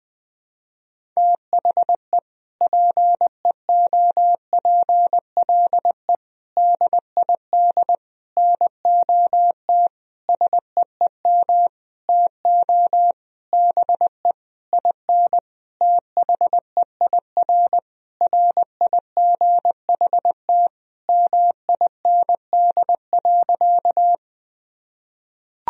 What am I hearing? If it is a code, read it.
Morse code, heard as a keyed tone, about 20 wpm, 700 Hz. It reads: THE PEOPLE DID NOT SEEM TO BE IN THEIR RIGHT MIND.
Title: Sentences from Top 1000 Words - Review Part 1 - 20wpm